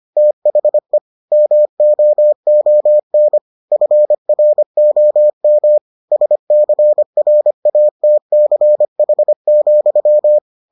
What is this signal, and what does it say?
Morse code, heard as a keyed tone, about 25 wpm, 600 Hz. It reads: THE MOON FROM SCRATCH,